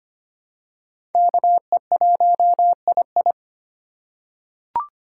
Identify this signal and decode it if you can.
Morse code, heard as a keyed tone, about 25 wpm, 700 Hz. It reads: KE1II
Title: USA Call Signs - 5 Characters - 2x Character Spacing - 25wpm